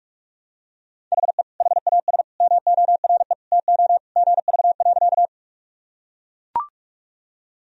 Morse code, heard as a keyed tone, about 41 wpm, 700 Hz. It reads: HE HAS MORE TO GV.
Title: Sentences from Top 200 Words and Abbreviations 45wpm